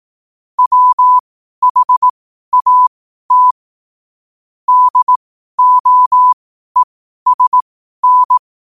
Morse code, heard as a keyed tone, about 18 wpm, spaced as for 12 wpm, 1000 Hz. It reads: WHAT DOESN
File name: qotdfiles / 12wpm